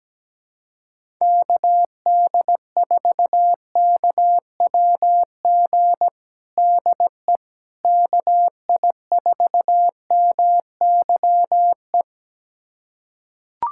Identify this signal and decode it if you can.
Morse code, heard as a keyed tone, about 17 wpm, 700 Hz. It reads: KD4KWG DE KI4MYE
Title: USA Callsign DE Callsign 17wpm